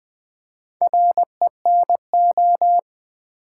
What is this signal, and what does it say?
Morse code, heard as a keyed tone, about 20 wpm, 700 Hz. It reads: RENO
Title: Build A Word - 4 Letters At A Time - 8-Letter Words - 20wpm